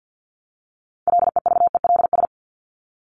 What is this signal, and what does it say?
Morse code, heard as a keyed tone, about 50 wpm, 700 Hz. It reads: LEVELS
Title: ES:S Suffix - Top 100 Words with Base Words 50wpm